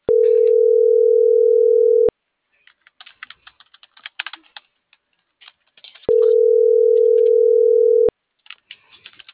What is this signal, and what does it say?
Ringback tone (North America), 440 + 480 Hz.
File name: CENTRAL NMP / out melanie.diaznmp - .43503